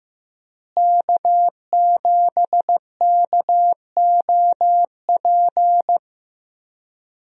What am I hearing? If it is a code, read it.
Morse code, heard as a keyed tone, about 15 wpm, 700 Hz. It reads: K7KOP
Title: USA Call Signs - 5 Characters - Repeated 2x - 15wpm